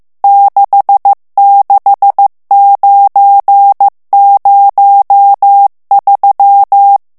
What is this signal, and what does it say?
Morse code, heard as a keyed tone, about 15 wpm, 800 Hz. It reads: 66903
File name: ITALIEN texte clair 1 vitesse 14 mots minute